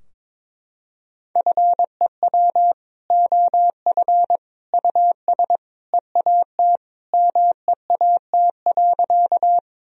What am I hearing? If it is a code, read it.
Morse code, heard as a keyed tone, about 22 wpm, 700 Hz. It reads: FEW OF US EAT MEAT.
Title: Sentences from Top 1000 Words - Review Part 3 - 22wpm